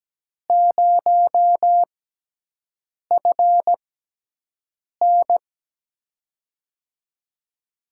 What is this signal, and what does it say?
Morse code, heard as a keyed tone, about 17 wpm, spaced as for 6 wpm, 700 Hz. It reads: 0FN
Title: USA Call Signs - 4 Characters - 6x Character Spacing - 17wpm